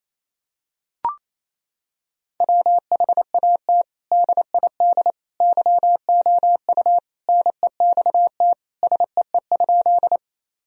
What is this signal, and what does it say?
Morse code, heard as a keyed tone, about 28 wpm, 700 Hz. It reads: WHAT DID YOU NEXT SEE?
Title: Sentences from Top 1000 Words - Review Part 1 - 28wpm